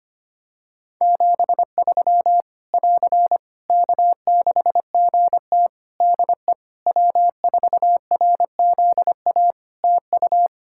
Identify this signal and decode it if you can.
Morse code, heard as a keyed tone, about 25 wpm, 700 Hz. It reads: 73 <AR> K6GT DE W4RZA TU
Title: Practice QSO - Exchange 4 - Ending - 25wpm